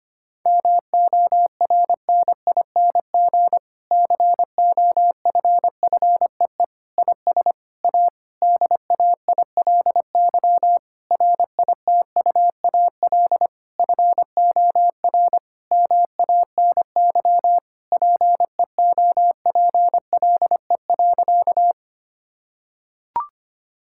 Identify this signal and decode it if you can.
Morse code, heard as a keyed tone, about 25 wpm, 700 Hz. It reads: MORNING COFFEE IS A DAILY RITUAL FOR MANY PEOPLE.
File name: practice / 25wpm / Difficult Sentences with Hints - 25wpm